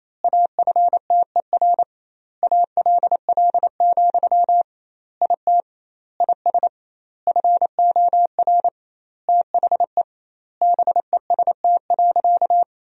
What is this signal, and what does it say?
Morse code, heard as a keyed tone, about 28 wpm, 700 Hz. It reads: AFTER ALL, IT IS FOR THE BEST.
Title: 2x Word-Spacing - Sentences from Top 500 Words - 28wpm